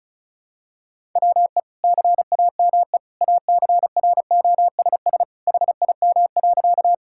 Morse code, heard as a keyed tone, about 36 wpm, 700 Hz. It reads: WE CAME ACROSS HIM.
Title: Sentences from Top 1000 Words - Review Part 5 - 35wpm